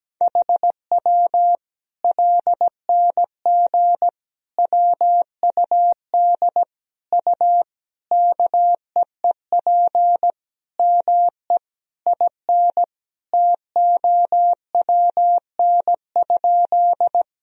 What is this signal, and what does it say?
Morse code, heard as a keyed tone, about 17 wpm, 700 Hz. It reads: HW LNG WUD U KEEP ME IN TOWN?
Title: Sentences from Top 500 Words and Abbreviations 17wpm